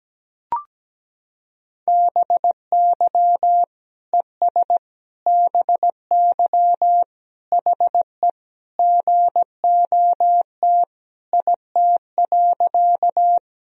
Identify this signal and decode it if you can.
Morse code, heard as a keyed tone, about 17 wpm, 700 Hz. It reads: BY ES BY HE GOT IT.
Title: Sentences from Top 300 Words and Abbreviations 17wpm